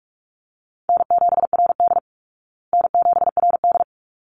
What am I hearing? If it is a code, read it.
Morse code, heard as a keyed tone, about 44 wpm, 700 Hz. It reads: N7RD N7RD
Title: Call Signs Repeated 3x - 45wpm